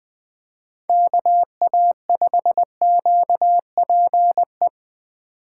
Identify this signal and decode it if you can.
Morse code, heard as a keyed tone, about 20 wpm, 700 Hz. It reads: KA5QPE